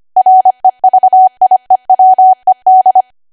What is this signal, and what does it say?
Morse code, heard as a keyed tone, about 25 wpm, 750 Hz. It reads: REVIEWED